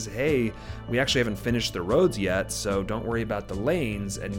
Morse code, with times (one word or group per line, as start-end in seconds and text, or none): none